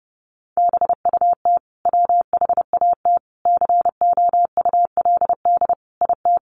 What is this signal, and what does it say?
Morse code, heard as a keyed tone, about 28 wpm, 700 Hz. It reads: BUT WHAT COULD IT
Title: Sentences from Top 1300 Words 30wpm